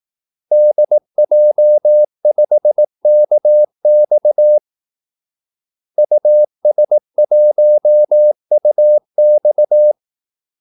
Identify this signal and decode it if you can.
Morse code, heard as a keyed tone, about 18 wpm, 600 Hz. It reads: DJ5KX US1UX
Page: Kaldesignaler 18 wpm